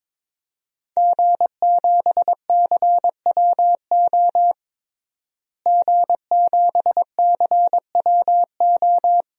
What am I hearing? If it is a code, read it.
Morse code, heard as a keyed tone, about 22 wpm, 700 Hz. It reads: G7CWO G7CWO
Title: DXCC-100 Call Signs - 5 Characters - Repeated 2x - 22wpm